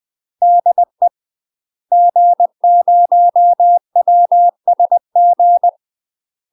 Morse code, heard as a keyed tone, about 20 wpm, 700 Hz. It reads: DE G0WSG
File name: UK MASTER QSOs TRAINING / 20WPM / UK MASTER QSOs 0017